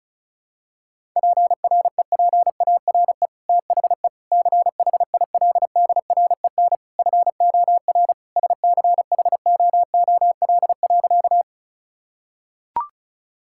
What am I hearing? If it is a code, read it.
Morse code, heard as a keyed tone, about 36 wpm, 700 Hz. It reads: PREPARE THE CHILDREN FOR SCHOOL.
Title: Sentences from Top 1000 Words - Review Part 6 - 35wpm